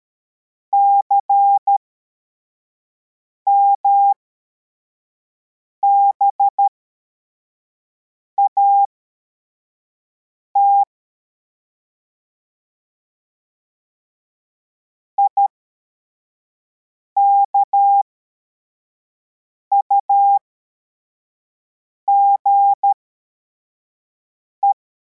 Morse code, heard as a keyed tone, about 13 wpm, spaced as for 4 wpm, 800 Hz. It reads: CMBAT IKUGE
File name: formation / cw / Text43